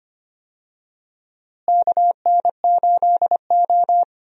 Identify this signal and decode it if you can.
Morse code, heard as a keyed tone, about 25 wpm, 700 Hz. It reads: KN8O
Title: USA Call Signs - 4 Characters - Repeated 2x - 25wpm